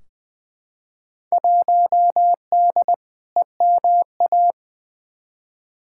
Morse code, heard as a keyed tone, about 20 wpm, 700 Hz. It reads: 1D EMA
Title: Field Day Exchanges - 20wpm